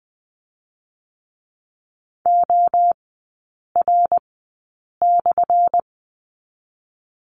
Morse code, heard as a keyed tone, about 20 wpm, 700 Hz. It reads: O R /
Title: 198 Lesson 32e - Letter k - Three Characters 20wpm